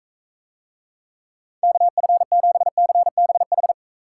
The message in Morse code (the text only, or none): KF7CBH